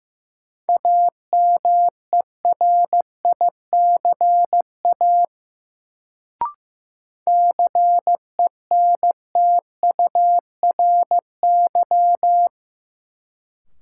Morse code, heard as a keyed tone, about 15 wpm, 700 Hz. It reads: AMERICA CENTURY
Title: Sets of 5 Words - New Word Drill - 15wpm